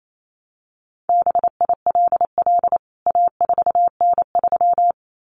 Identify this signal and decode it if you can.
Morse code, heard as a keyed tone, about 28 wpm, 700 Hz. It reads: BILL A4N3
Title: CWT Exchange 28wpm